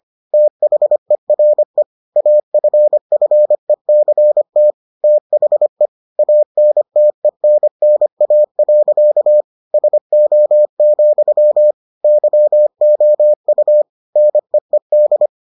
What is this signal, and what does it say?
Morse code, heard as a keyed tone, about 25 wpm, 600 Hz. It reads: THERE AFFECT THE ANTENNA. SO, YOU NEED